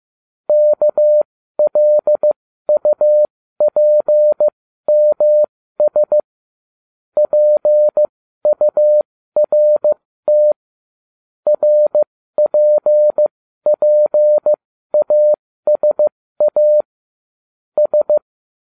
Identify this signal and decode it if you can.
Morse code, heard as a keyed tone, about 15 wpm, spaced as for 12 wpm, 600 Hz. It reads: KLUPMS PURT RPPASA S